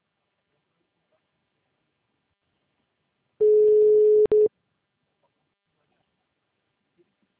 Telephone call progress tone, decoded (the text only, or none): ringback tone